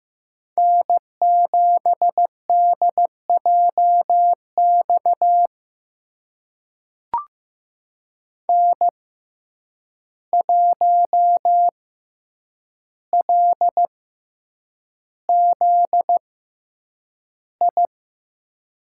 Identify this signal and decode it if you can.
Morse code, heard as a keyed tone, about 15 wpm, spaced as for 8 wpm, 700 Hz. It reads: N7DJX N1LZI
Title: USA Call Signs - 5 Characters - 6x Character Spacing - 15wpm